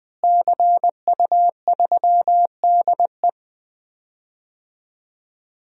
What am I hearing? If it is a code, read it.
Morse code, heard as a keyed tone, about 20 wpm, 700 Hz. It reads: CU3DE
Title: DXCC-100 Call Signs - 5 Characters - Repeated 3x - 20wpm